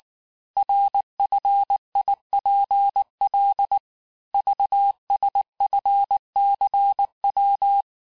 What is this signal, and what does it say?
Morse code, heard as a keyed tone, about 19 wpm, 800 Hz. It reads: RFIPL VSFCW
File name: cours-du-lundi files / cours-1595-teste-CHIFFRE-vitesse18-MPM